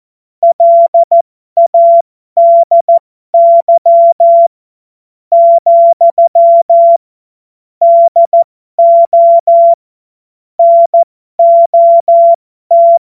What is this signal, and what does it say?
Morse code, heard as a keyed tone, about 14 wpm, spaced as for 11 wpm, 680 Hz. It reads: LADY, DO NOT
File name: Morsekurs / Witze